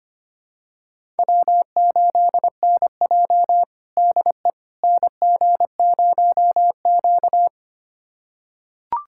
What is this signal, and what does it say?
Morse code, heard as a keyed tone, about 25 wpm, 700 Hz. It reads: W8NJ DE NG0Q
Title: USA Callsign DE Callsign - 4 Characters - 25wpm